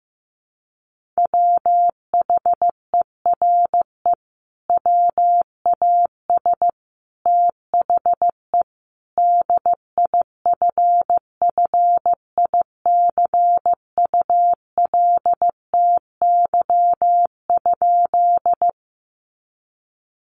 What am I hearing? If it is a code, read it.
Morse code, heard as a keyed tone, about 15 wpm, 700 Hz. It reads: WHERE WAS THE DIFFICULTY?
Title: Sentences from Top 1500 Words 15wpm